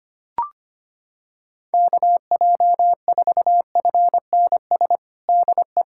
Morse code, heard as a keyed tone, about 25 wpm, 700 Hz. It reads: KJ4FNS DE
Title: USA Callsign DE Callsign 25wpm